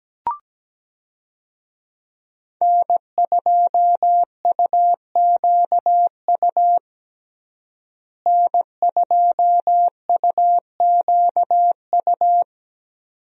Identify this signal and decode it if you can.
Morse code, heard as a keyed tone, about 17 wpm, 700 Hz. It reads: N2UQU N2UQU
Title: USA Call Signs - 5 Characters - Repeated 2x - 17wpm